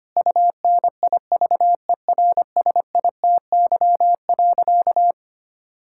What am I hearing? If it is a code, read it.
Morse code, heard as a keyed tone, about 25 wpm, 700 Hz. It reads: UNIVERSITY.
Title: Sentences from Top 1000 Words - Review Part 5 - 25wpm